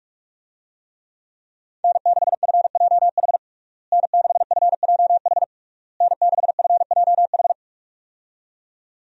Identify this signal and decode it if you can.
Morse code, heard as a keyed tone, about 42 wpm, 700 Hz. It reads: N6FJH N6FJH N6FJH
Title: USA Call Signs - 5 Characters - Repeated 3x - 45wpm